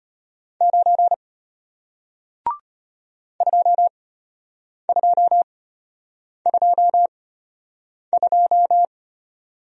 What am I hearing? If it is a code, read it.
Morse code, heard as a keyed tone, about 34 wpm, spaced as for 8 wpm, 700 Hz. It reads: 9 2222